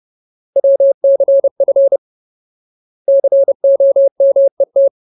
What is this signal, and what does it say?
Morse code, heard as a keyed tone, about 30 wpm, 550 Hz. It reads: WCF COMET
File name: Ultimate Word List / 30WPM Ultimate